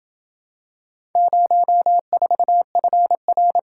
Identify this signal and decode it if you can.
Morse code, heard as a keyed tone, about 27 wpm, 700 Hz. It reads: 04FR